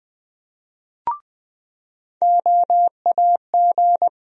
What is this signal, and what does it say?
Morse code, heard as a keyed tone, about 20 wpm, 700 Hz. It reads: OAG
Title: USA Call Sign Postfixes - 3 Letters 20wpm